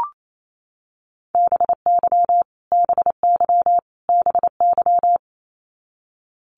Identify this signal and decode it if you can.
Morse code, heard as a keyed tone, about 28 wpm, 700 Hz. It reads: BY BY BY
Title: Top 300 Words Repeated 28wpm